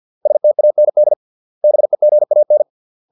Morse code, heard as a keyed tone, about 50 wpm, 600 Hz. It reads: STAND BEGAN